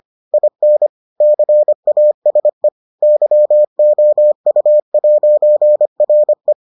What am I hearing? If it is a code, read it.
Morse code, heard as a keyed tone, about 25 wpm, 600 Hz. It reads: IN CASE YOU'RE